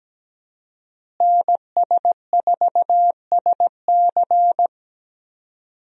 Morse code, heard as a keyed tone, about 17 wpm, 700 Hz. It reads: NS4SC
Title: USA Call Signs - 5 Characters - Repeated 2x - 17wpm